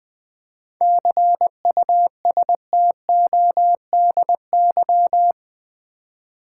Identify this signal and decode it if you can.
Morse code, heard as a keyed tone, about 20 wpm, 700 Hz. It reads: CUSTODY
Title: 7 Letter Words - First 3 Letters Given - 20wpm